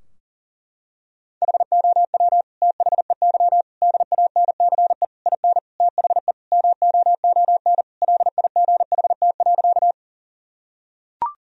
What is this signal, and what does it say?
Morse code, heard as a keyed tone, about 37 wpm, 700 Hz. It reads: HOW THEY DANCE IN THE MOON LIGHT.
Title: Sentences from Top 1000 Words - Review Part 4 - 40wpm